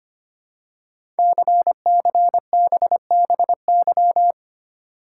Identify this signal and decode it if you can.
Morse code, heard as a keyed tone, about 25 wpm, 700 Hz. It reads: CCBBY
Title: Code Groups - 5 Letters - 25wpm